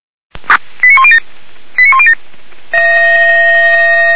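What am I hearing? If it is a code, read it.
Five-tone selective call (ZVEI), heard as quick runs of five tones, tones of about 70 ms. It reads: 89107 89107
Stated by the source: ZVEI five-tone selective call received by SDR, from feuerwehr-orlamuende fire brigade server.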